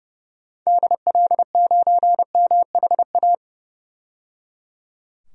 Morse code, heard as a keyed tone, about 30 wpm, 700 Hz. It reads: DL9MHA
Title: DXCC-100 Call Signs - 6 Characters - 30wpm